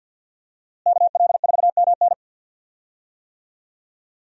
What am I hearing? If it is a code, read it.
Morse code, heard as a keyed tone, about 51 wpm, 700 Hz. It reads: KL4MN
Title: USA Call Signs - 5 Characters - Repeated 2x - 50wpm